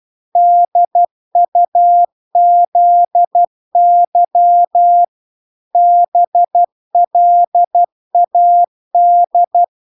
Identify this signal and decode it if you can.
Morse code, heard as a keyed tone, about 12 wpm, 700 Hz. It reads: DUZY BLAD